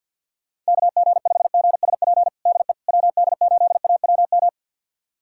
Morse code, heard as a keyed tone, about 50 wpm, 700 Hz. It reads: KG5GSP DE WD8AWM
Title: USA Callsign DE Callsign 50wpm